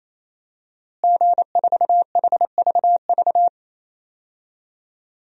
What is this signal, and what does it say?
Morse code, heard as a keyed tone, about 28 wpm, 700 Hz. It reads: G4HVV
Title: DXCC-100 Call Signs - 5 Characters - Repeated 3x - 28wpm